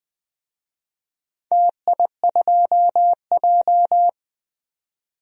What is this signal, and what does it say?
Morse code, heard as a keyed tone, about 20 wpm, 700 Hz. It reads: TI2J
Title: DXCC-100 Call Signs - 4 Characters - 20wpm